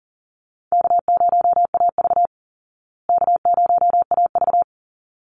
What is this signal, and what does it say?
Morse code, heard as a keyed tone, about 40 wpm, 700 Hz. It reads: K0AV K0AV
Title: Easy Call Signs Repeated 2x - 40wpm